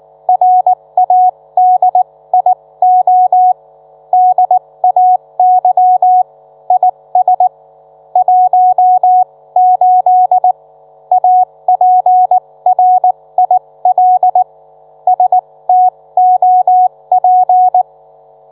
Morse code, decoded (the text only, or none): RADIO DAY IS 18 APRIL STOP